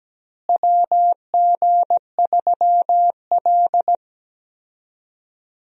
Morse code, heard as a keyed tone, about 17 wpm, 700 Hz. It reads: WG3L
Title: USA Call Signs - 4 Characters - Repeated 3x - 17wpm